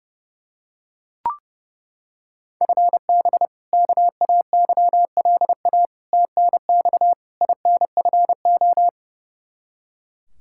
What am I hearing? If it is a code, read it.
Morse code, heard as a keyed tone, about 30 wpm, 700 Hz. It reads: FB KAYLA TNX INFO
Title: K1USN SST QSO Practice 30wpm